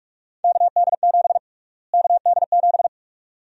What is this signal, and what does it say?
Morse code, heard as a keyed tone, about 46 wpm, 700 Hz. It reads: KD7 KD7
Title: USA Call Sign Prefixes - Repeated 2x - 45wpm